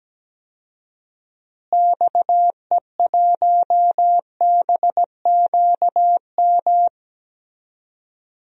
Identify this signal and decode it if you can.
Morse code, heard as a keyed tone, about 17 wpm, 700 Hz. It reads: XE1BQM
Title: DXCC-100 Call Signs - 6 Characters - Repeated 2x - 17wpm